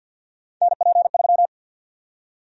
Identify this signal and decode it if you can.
Morse code, heard as a keyed tone, about 52 wpm, 700 Hz. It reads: NP3
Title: USA Call Sign Prefixes - Repeated 2x - 50wpm